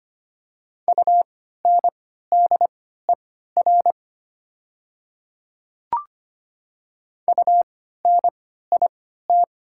Morse code, heard as a keyed tone, about 25 wpm, spaced as for 14 wpm, 690 Hz. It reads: UNDER UNIT
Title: 3x Character Spacing - Top 500 Words - 25wpm